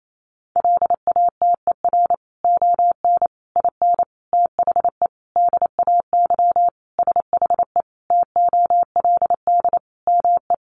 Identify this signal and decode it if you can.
Morse code, heard as a keyed tone, about 28 wpm, 700 Hz. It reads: LATER ON IN THE DAY SHE TOLD ME
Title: Sentences from Top 700 Words 28wpm